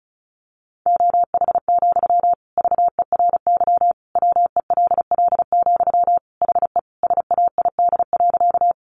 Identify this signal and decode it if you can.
Morse code, heard as a keyed tone, about 34 wpm, 700 Hz. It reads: OH, VERY WELL, HE SAID.
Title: Sentences from Top 500 Words 35wpm